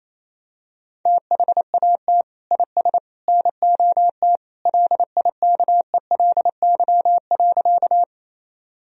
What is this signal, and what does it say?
Morse code, heard as a keyed tone, about 27 wpm, 700 Hz. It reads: THAT IS NOT LIKELY.
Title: Sentences from Top 1000 Words - Review Part 4 - 28wpm